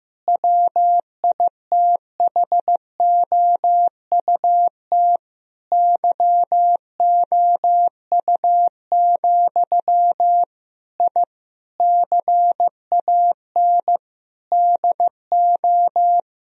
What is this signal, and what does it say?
Morse code, heard as a keyed tone, about 15 wpm, 700 Hz. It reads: WITHOUT YOU, I CAN DO
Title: Sentences from Top 1000 Words - Review Part 3 - 15wpm